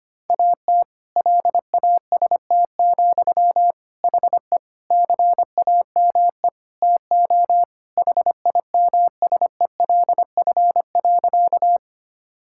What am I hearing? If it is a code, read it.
Morse code, heard as a keyed tone, about 25 wpm, 700 Hz. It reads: AT LAST, HE CAME TO HIMSELF.
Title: Sentences from Top 1600 Words 25wpm